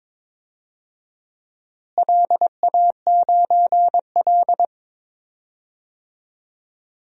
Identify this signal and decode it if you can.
Morse code, heard as a keyed tone, about 22 wpm, 700 Hz. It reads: LA9L
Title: DXCC-100 Call Signs - 4 Characters - Repeated 2x - 22wpm